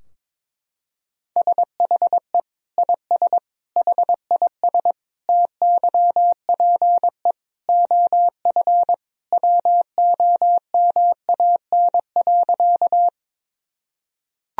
Morse code, heard as a keyed tone, about 22 wpm, 700 Hz. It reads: SHE IS HIS TYPE OF WOMAN.
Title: Sentences from Top 1000 Words - Review Part 3 - 22wpm